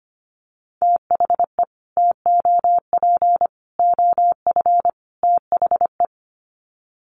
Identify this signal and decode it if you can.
Morse code, heard as a keyed tone, about 25 wpm, 700 Hz. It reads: THE TOP OF THE
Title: Sets of 4 Words 25wpm